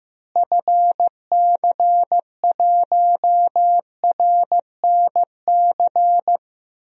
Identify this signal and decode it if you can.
Morse code, heard as a keyed tone, about 15 wpm, 700 Hz. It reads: FC1RNC
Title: DXCC-100 Call Signs - 6 Characters - Repeated 3x - 15wpm